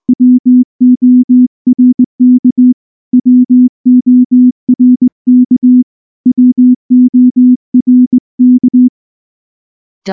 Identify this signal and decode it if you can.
Morse code, heard as a keyed tone, about 19 wpm, 250 Hz. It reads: WORK WORK WORK
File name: morse-training-20WPM